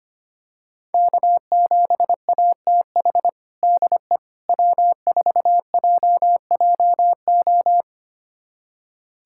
Morse code, heard as a keyed tone, about 25 wpm, 700 Hz. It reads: K7ATH DE W4JJO
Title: USA Callsign DE Callsign - 5 Characters - 25wpm